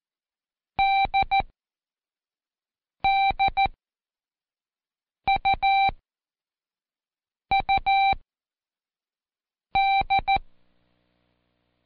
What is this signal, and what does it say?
Morse code, heard as a keyed tone, about 14 wpm, spaced as for 5 wpm, 780 Hz. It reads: DDUUD
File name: Code Quick / AudioDownld1 / Seg23-Day3-Intro D